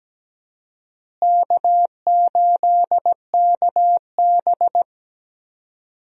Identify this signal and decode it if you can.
Morse code, heard as a keyed tone, about 17 wpm, 700 Hz. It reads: K8KB
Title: USA Call Signs - 4 Characters - Repeated 3x - 17wpm